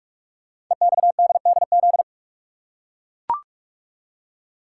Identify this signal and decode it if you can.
Morse code, heard as a keyed tone, about 44 wpm, 700 Hz. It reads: EXDDZ
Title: Code Groups - 5 Letters - 45wpm